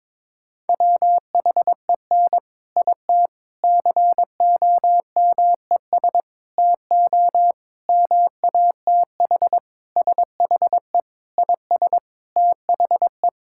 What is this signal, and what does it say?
Morse code, heard as a keyed tone, about 22 wpm, 700 Hz. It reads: WHEN IT COMES TO MATH SHE IS THE
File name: practice / 22wpm / Sentences with 4-Word Phrases - 22wpm